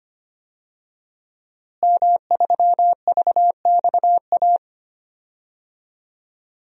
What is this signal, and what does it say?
Morse code, heard as a keyed tone, about 25 wpm, 700 Hz. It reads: M3VXA
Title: DXCC-100 Call Signs - 5 Characters - Repeated 2x - 25wpm